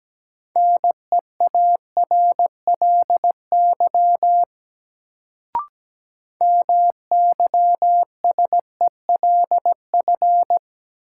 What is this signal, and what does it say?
Morse code, heard as a keyed tone, about 17 wpm, 700 Hz. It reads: NEARLY MYSELF
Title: Sentences from Top 1300 Words - New Word Drill Alt - 17wpm